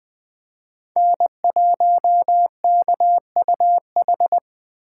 Morse code, heard as a keyed tone, about 20 wpm, 700 Hz. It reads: N1KUH